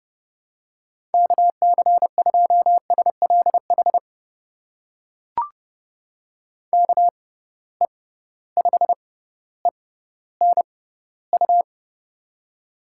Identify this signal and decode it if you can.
Morse code, heard as a keyed tone, about 32 wpm, spaced as for 16 wpm, 700 Hz. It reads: KC2SLH KE5ENU
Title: USA Call Signs - 6 Characters - 6x Character Spacing - 30wpm